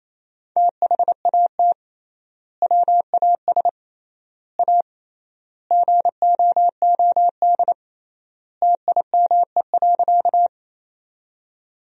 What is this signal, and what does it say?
Morse code, heard as a keyed tone, about 28 wpm, 700 Hz. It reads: THAT WAS A GOOD TIME.
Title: 3x Word-Spacing - Sentences from Top 200 Words - 28wpm